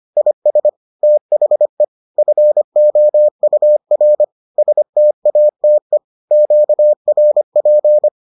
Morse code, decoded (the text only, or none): IS THE FOUR STATE QRP